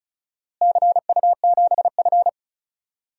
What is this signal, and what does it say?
Morse code, heard as a keyed tone, about 36 wpm, 700 Hz. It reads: CU7F
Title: DXCC-100 Call Signs - 4 Characters - Repeated 2x - 35wpm